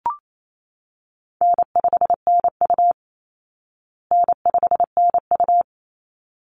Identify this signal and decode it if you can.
Morse code, heard as a keyed tone, about 28 wpm, 700 Hz. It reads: N5NU N5NU
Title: Easy Call Signs Repeated 2x - 28wpm